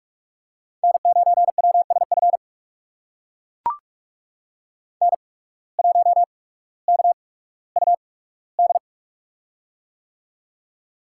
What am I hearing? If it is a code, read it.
Morse code, heard as a keyed tone, about 42 wpm, spaced as for 19 wpm, 700 Hz. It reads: N9WSF N1KUD
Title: USA Call Signs - 5 Characters - 8x Character Spacing - 45wpm